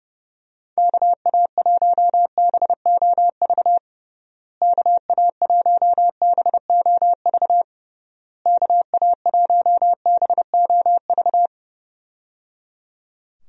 Morse code, heard as a keyed tone, about 30 wpm, 700 Hz. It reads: KA1BOV KA1BOV KA1BOV
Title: USA Call Signs - 6 Characters - Repeated 3x - 30wpm